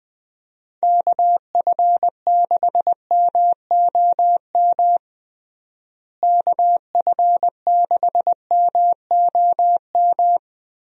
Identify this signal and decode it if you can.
Morse code, heard as a keyed tone, about 21 wpm, 700 Hz. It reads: KF6MOM KF6MOM
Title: USA Call Signs - 6 Characters - Repeated 3x - 20wpm